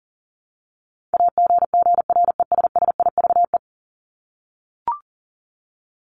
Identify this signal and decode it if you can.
Morse code, heard as a keyed tone, about 40 wpm, 700 Hz. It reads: AGGRESSIVE
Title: Top 1501-1750 Words 40wpm